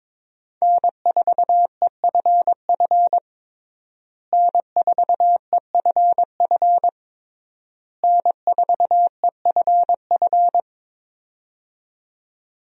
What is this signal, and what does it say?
Morse code, heard as a keyed tone, about 22 wpm, 700 Hz. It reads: N4EFF N4EFF N4EFF
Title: USA Call Signs - 5 Characters - Repeated 3x - 22wpm